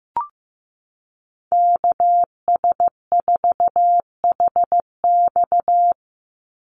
Morse code, heard as a keyed tone, about 15 wpm, 700 Hz. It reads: KS4HX
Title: USA Call Signs 15wpm